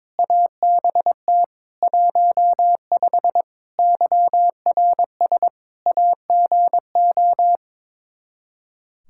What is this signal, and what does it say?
Morse code, heard as a keyed tone, about 22 wpm, 700 Hz. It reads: ABT 15 YRS AGO
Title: Practice QSO - Exchange 3 - More Info - 22wpm